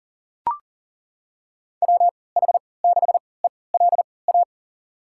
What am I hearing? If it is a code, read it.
Morse code, heard as a keyed tone, about 38 wpm, spaced as for 22 wpm, 700 Hz. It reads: WH6ELA
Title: USA Call Signs - 6 Characters - 3x Character Spacing - 40wpm